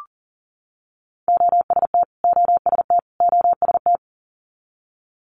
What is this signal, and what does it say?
Morse code, heard as a keyed tone, about 40 wpm, 700 Hz. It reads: OST OST OST
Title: Multi-letter Phonemes Repeated 3x 40wpm